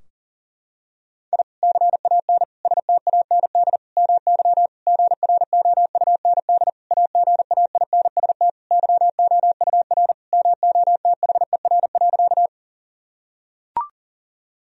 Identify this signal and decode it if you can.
Morse code, heard as a keyed tone, about 40 wpm, 700 Hz. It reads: I CAN STAND MY GROUND AGAINST YOUR MOTHER.
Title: Sentences from Top 1000 Words - Review Part 2 - 40wpm